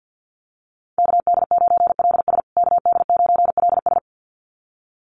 Morse code, heard as a keyed tone, about 50 wpm, 700 Hz. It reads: KD9LS KD9LS
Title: USA Call Signs Repeated 2x 50wpm